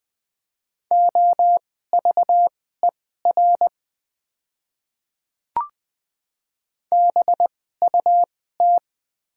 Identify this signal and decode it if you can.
Morse code, heard as a keyed tone, about 20 wpm, spaced as for 14 wpm, 700 Hz. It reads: OVER BUT
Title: 2x Character Spacing - Top 500 Words - 20wpm